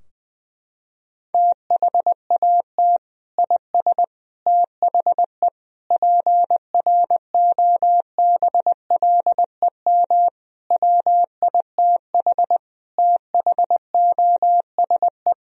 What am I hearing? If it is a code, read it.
Morse code, heard as a keyed tone, about 20 wpm, 700 Hz. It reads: THAT IS THE PROBLEM WITH THOSE
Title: Sentences from Top 1000 Words - Review Part 2 - 20wpm